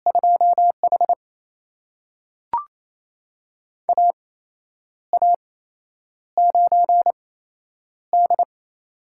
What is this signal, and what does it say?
Morse code, heard as a keyed tone, about 28 wpm, spaced as for 9 wpm, 700 Hz. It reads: 2H AA9D
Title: USA Call Signs - 4 Characters - 8x Character Spacing - 28wpm